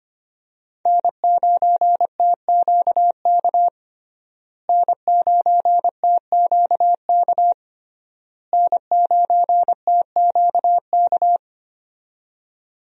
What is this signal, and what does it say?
Morse code, heard as a keyed tone, about 25 wpm, 700 Hz. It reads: N9TQK N9TQK N9TQK
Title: USA Call Signs - 5 Characters - Repeated 3x - 25wpm